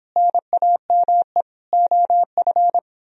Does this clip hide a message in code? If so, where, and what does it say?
Yes: Morse code from 0.2 to 2.8 s, heard as a keyed tone, about 29 wpm, 700 Hz. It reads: NAME OF